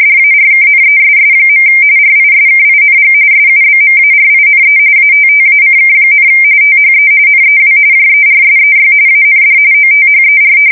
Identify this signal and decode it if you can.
RTTY, about 45.45 baud. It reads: DARKNESS. RISKING SUBMARINE ATTACK, THE NEW JERSEY AND OTHER
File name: fsk170 / USS-New-Jersey